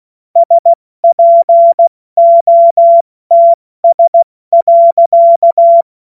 Morse code, heard as a keyed tone, about 16 wpm, 680 Hz. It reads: SPOTS.
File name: Morsekurs / Witze / Witz 27. April 2021, 70 F80